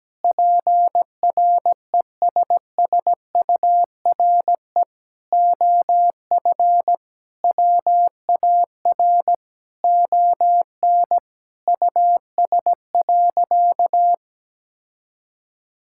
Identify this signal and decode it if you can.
Morse code, heard as a keyed tone, about 17 wpm, 700 Hz. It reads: PRESSURE OF WAR ON US.
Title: Sentences from Top 1000 Words - Review Part 6 - 17wpm